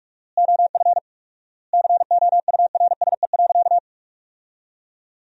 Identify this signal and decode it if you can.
Morse code, heard as a keyed tone, about 46 wpm, 700 Hz. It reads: OF COURSE.
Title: 4x Word-Spacing - Sentences from Top 500 Words - 45wpm